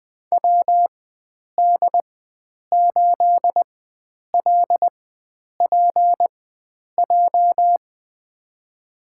Morse code, heard as a keyed tone, about 20 wpm, spaced as for 9 wpm, 700 Hz. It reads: WD8LPJ
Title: USA Call Signs - 6 Characters - 4x Character Spacing - 20wpm